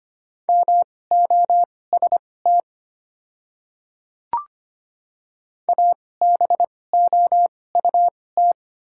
Morse code, heard as a keyed tone, about 25 wpm, spaced as for 18 wpm, 690 Hz. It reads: MOST ABOUT
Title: 2x Character Spacing - Top 100 Words - 25wpm